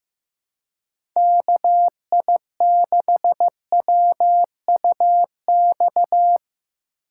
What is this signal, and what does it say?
Morse code, heard as a keyed tone, about 15 wpm, 700 Hz. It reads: KI6WUX
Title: USA Call Signs - 6 Characters - 3x Character Spacing - 15wpm